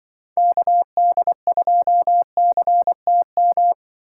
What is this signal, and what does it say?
Morse code, heard as a keyed tone, about 25 wpm, 700 Hz. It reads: KD2CTM
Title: Speed Racing - USA 2x3 Call Signs 20wpm